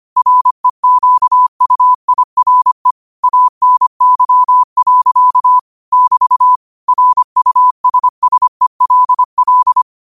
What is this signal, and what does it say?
Morse code, heard as a keyed tone, about 25 wpm, 1000 Hz. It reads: REQUIRE ANY. <BT> RUSSELL
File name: qotdfiles / 25wpm